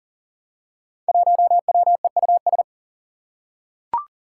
Morse code, heard as a keyed tone, about 42 wpm, 700 Hz. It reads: 1WEUS
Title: Five Challenging Letter-Number Combinations 40wpm